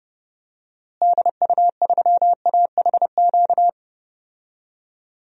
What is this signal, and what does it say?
Morse code, heard as a keyed tone, about 30 wpm, 700 Hz. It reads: DU3AHQ
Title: DXCC-100 Call Signs - 6 Characters - Repeated 2x - 30wpm